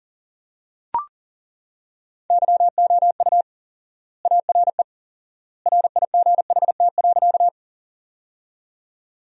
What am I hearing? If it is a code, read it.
Morse code, heard as a keyed tone, about 40 wpm, 700 Hz. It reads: YOU ARE RIGHT.
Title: 4x Word-Spacing - Sentences from Top 200 Words - 40wpm